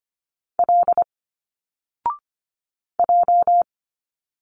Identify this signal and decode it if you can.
Morse code, heard as a keyed tone, about 25 wpm, 700 Hz. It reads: L J
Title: Single Letters 25wpm